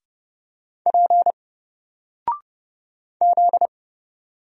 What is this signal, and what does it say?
Morse code, heard as a keyed tone, about 29 wpm, 700 Hz. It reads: P Z
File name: Single-Letter-Number-30wpm